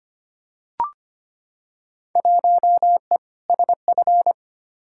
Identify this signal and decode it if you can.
Morse code, heard as a keyed tone, about 25 wpm, 700 Hz. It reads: 1E SF
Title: Field Day Exchanges - 25wpm